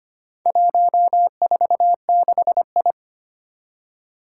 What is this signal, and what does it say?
Morse code, heard as a keyed tone, about 25 wpm, 700 Hz. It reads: 146I
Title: Four Challenging Letter-Number Combinations 25wpm